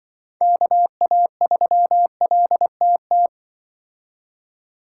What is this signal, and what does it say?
Morse code, heard as a keyed tone, about 24 wpm, 700 Hz. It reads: KA3LTT
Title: Speed Racing - USA 2x3 Call Signs 20wpm